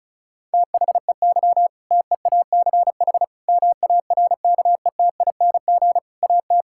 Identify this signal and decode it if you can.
Morse code, heard as a keyed tone, about 34 wpm, 700 Hz. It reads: THEY TEACH MARKETING AT
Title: Sentences from Top 1000 Words - Review Part 5 - 35wpm